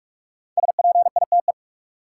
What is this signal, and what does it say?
Morse code, heard as a keyed tone, about 43 wpm, 700 Hz. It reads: SPITE
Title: Five Letter Words - 45wpm